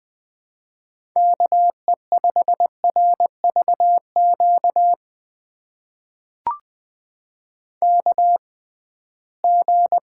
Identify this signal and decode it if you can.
Morse code, heard as a keyed tone, about 20 wpm, spaced as for 15 wpm, 700 Hz. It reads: KE5RVQ KG
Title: USA Call Signs - 6 Characters - 6x Character Spacing - 20wpm